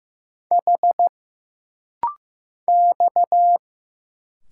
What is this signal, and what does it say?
Morse code, heard as a keyed tone, about 15 wpm, 700 Hz. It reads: H X
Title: Single Letters - Rapid-Fire 15wpm